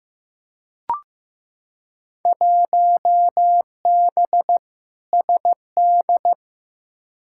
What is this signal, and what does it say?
Morse code, heard as a keyed tone, about 15 wpm, 700 Hz. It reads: 1B SD
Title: Field Day Exchanges Encore - 15wpm